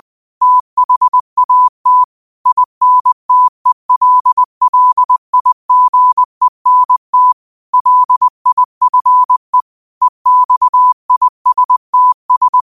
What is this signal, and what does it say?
Morse code, heard as a keyed tone, about 20 wpm, 1000 Hz. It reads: THAT INTELLIGENT LIFE EXISTS